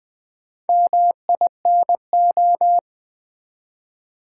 Morse code, heard as a keyed tone, about 20 wpm, 700 Hz. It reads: MINO
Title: Build A Word - 1 Letter At A Time - 3 to 5-Letter Words - 20wpm